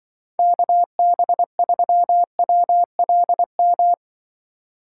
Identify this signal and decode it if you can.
Morse code, heard as a keyed tone, about 24 wpm, 700 Hz. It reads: KB3WLM
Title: Speed Racing - USA 2x3 Call Signs 20wpm